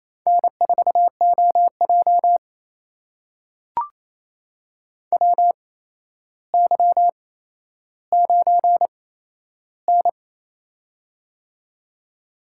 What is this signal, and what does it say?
Morse code, heard as a keyed tone, about 28 wpm, spaced as for 12 wpm, 700 Hz. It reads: N4OJ WY9N
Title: USA Call Signs - 4 Characters - 8x Character Spacing - 28wpm